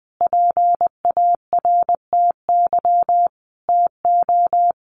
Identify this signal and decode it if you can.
Morse code, heard as a keyed tone, about 20 wpm, 700 Hz. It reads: PARTY TO
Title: Sentences from Top 1400 Words 20wpm